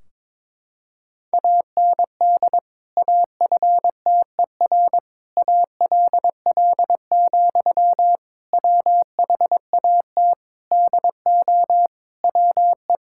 Morse code, heard as a keyed tone, about 22 wpm, 700 Hz. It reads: AND AFTER ALL, WHAT DO WE
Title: Sentences from Top 1600 Words 22wpm